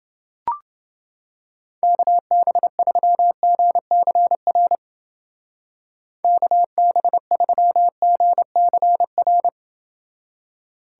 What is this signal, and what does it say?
Morse code, heard as a keyed tone, about 28 wpm, 700 Hz. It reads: KB3GCR KB3GCR